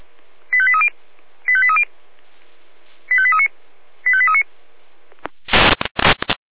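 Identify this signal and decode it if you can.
Five-tone selective call (ZVEI), heard as quick runs of five tones, tones of about 70 ms. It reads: 85830 85830 85839 85839
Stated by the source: ZVEI five-tone selective call received by SDR.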